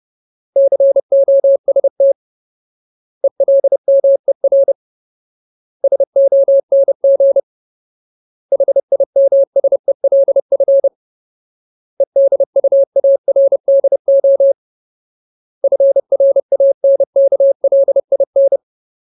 Morse code, heard as a keyed tone, about 30 wpm, 550 Hz. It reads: COST ELMER SONG HIMSELF EDUARDO FRANKLIN
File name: Ultimate Word List / 30WPM Ultimate